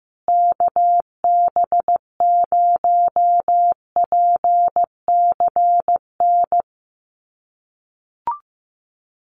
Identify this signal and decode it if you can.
Morse code, heard as a keyed tone, about 15 wpm, 700 Hz. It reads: KB0PCN
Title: USA Call Signs Repeated 2x 15wpm